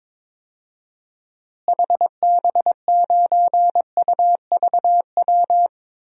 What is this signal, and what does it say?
Morse code, heard as a keyed tone, about 22 wpm, 700 Hz. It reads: HB9UVW
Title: DXCC-100 Call Signs - 6 Characters - Repeated 2x - 22wpm